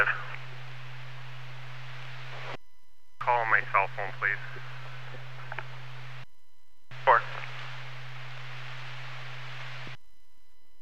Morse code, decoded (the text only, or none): none